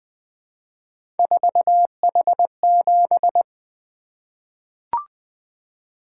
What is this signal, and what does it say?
Morse code, heard as a keyed tone, about 20 wpm, 700 Hz. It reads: 4H7
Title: Three Letter-Number Combinations - 20wpm